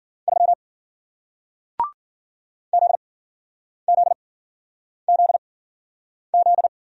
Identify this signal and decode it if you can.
Morse code, heard as a keyed tone, about 53 wpm, spaced as for 9 wpm, 700 Hz. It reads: 3 ZZZZ